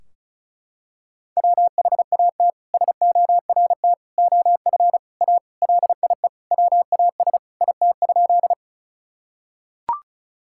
Morse code, heard as a keyed tone, about 34 wpm, 700 Hz. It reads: WHAT SORT OF A LIE WAS IT?
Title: Sentences from Top 1600 Words 35wpm